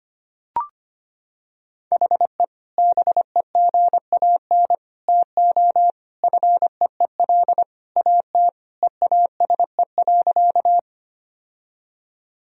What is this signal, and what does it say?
Morse code, heard as a keyed tone, about 25 wpm, 700 Hz. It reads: HE BEGAN TO FEEL AT EASE.
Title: Sentences from Top 1000 Words - Review Part 1 - 25wpm